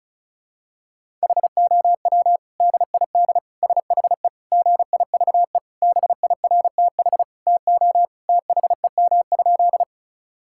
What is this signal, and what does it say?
Morse code, heard as a keyed tone, about 36 wpm, 700 Hz. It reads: HOW DID SHE GIVE BIRTH TO THEM?
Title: Sentences from Top 1000 Words - Review Part 5 - 35wpm